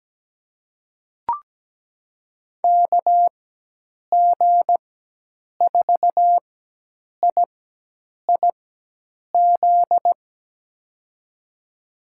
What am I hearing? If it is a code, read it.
Morse code, heard as a keyed tone, about 17 wpm, spaced as for 8 wpm, 700 Hz. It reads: KG4IIZ
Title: USA Call Signs - 6 Characters - 4x Character Spacing - 17wpm